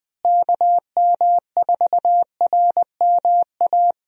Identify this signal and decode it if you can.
Morse code, heard as a keyed tone, about 20 wpm, 700 Hz. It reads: KM4RMA
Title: USA Call Signs - 6 Characters - Repeated 2x - 20wpm